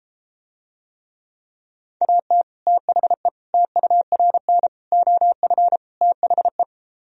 Morse code, heard as a keyed tone, about 33 wpm, 700 Hz. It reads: AT THE TURN OF THE